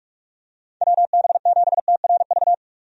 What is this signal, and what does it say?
Morse code, heard as a keyed tone, about 44 wpm, 700 Hz. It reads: WB7TRV
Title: USA Call Signs - 6 Characters - 45wpm